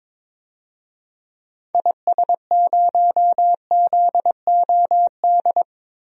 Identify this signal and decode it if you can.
Morse code, heard as a keyed tone, about 22 wpm, 700 Hz. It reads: IS0ZOD
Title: DXCC-100 Call Signs - 6 Characters - 22wpm